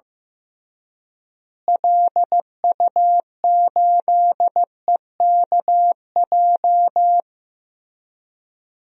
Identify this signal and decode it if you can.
Morse code, heard as a keyed tone, about 15 wpm, 700 Hz. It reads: LU8EKJ
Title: DXCC-100 Call Signs - 6 Characters - Repeated 3x - 15wpm